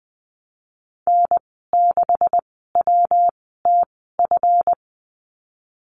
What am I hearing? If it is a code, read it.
Morse code, heard as a keyed tone, about 20 wpm, spaced as for 15 wpm, 700 Hz. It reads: N6WTF
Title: 140 Lesson 23g - Letter f - Call Signs 20wpm